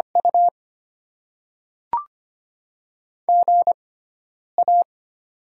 Morse code, heard as a keyed tone, about 25 wpm, spaced as for 9 wpm, 700 Hz. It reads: U GA
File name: Top 20 QSO Elements - 6x Character Spacing - 25wpm